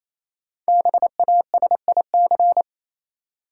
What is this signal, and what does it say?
Morse code, heard as a keyed tone, about 28 wpm, 700 Hz. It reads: BASIC
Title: Sentences from Top 1000 Words - Review Part 3 - New Word Drill - 28wpm